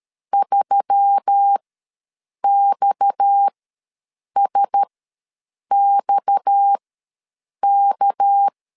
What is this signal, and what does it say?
Morse code, heard as a keyed tone, about 13 wpm, spaced as for 7 wpm, 800 Hz. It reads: 3XSXK
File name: LEZIONI / 6 PKX0 / 6 PKX ws 12wpm 3106